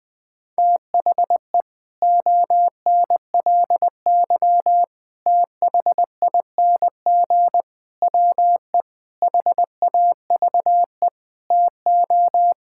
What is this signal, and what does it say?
Morse code, heard as a keyed tone, about 20 wpm, 700 Hz. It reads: THE ONLY THING WE HAVE TO
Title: Sets of 8 Words - 20wpm